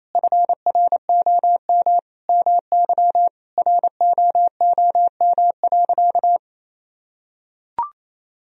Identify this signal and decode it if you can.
Morse code, heard as a keyed tone, about 28 wpm, 700 Hz. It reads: FROM MY ROOM.
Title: Sentences from Top 1000 Words - Review Part 1 - 28wpm